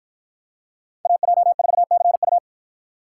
Morse code, heard as a keyed tone, about 53 wpm, 700 Hz. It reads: AJ4CU